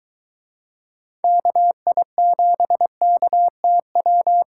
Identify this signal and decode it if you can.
Morse code, heard as a keyed tone, about 22 wpm, 700 Hz. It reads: KI7KTW